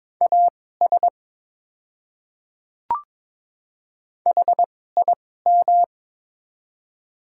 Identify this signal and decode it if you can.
Morse code, heard as a keyed tone, about 22 wpm, spaced as for 16 wpm, 700 Hz. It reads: AS HIM
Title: 2x Character Spacing - Top 100 Words - 22wpm